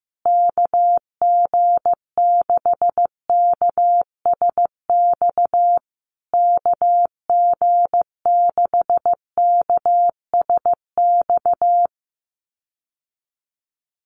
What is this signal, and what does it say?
Morse code, heard as a keyed tone, about 15 wpm, 700 Hz. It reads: KG6KSX KG6KSX
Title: USA Call Signs Repeated 3x 15wpm